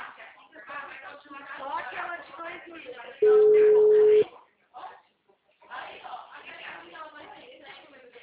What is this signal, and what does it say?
Ringback tone (Europe, 425 Hz), with speech.